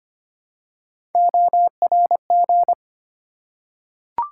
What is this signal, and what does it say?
Morse code, heard as a keyed tone, about 25 wpm, 700 Hz. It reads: ORG
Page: morse-code-ninja-practice-25wpm